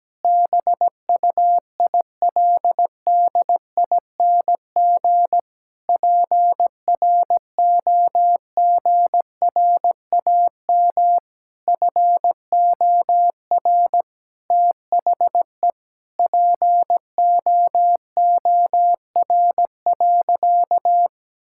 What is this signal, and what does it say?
Morse code, heard as a keyed tone, about 17 wpm, 700 Hz. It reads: BUILDING PROGRAM FOR THE POOR.
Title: Sentences from Top 1000 Words - Review Part 5 - 17wpm